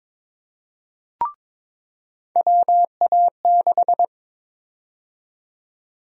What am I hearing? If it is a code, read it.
Morse code, heard as a keyed tone, about 22 wpm, 700 Hz. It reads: WA6
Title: USA Call Sign Prefixes - 22wpm